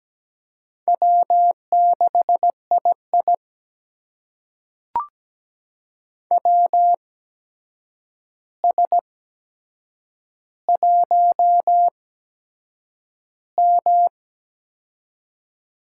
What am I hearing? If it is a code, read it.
Morse code, heard as a keyed tone, about 17 wpm, spaced as for 7 wpm, 700 Hz. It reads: W6II WS1M
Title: USA Call Signs - 4 Characters - 8x Character Spacing - 17wpm